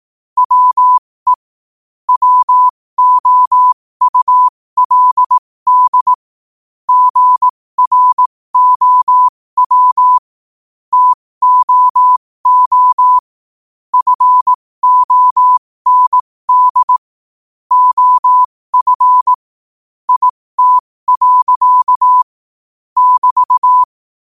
Morse code, heard as a keyed tone, about 18 wpm, spaced as for 15 wpm, 1000 Hz. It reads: WE WOULD GROW TOO FOND OF IT. <BT>
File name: qotdfiles / 15wpm